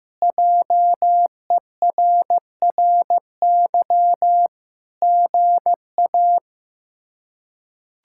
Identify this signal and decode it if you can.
Morse code, heard as a keyed tone, about 16 wpm, 700 Hz. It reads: JERRY GA
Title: K1USN SST QSO Practice - 15wpm